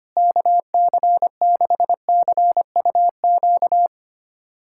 Morse code, heard as a keyed tone, about 25 wpm, 700 Hz. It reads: KC6CUQ
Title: USA Callsign DE Callsign 25wpm